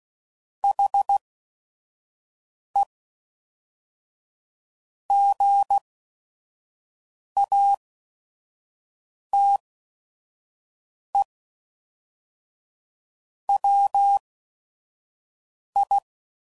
Morse code, heard as a keyed tone, about 16 wpm, spaced as for 5 wpm, 800 Hz. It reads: HE GATE WI
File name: CW Code Groups @5WPM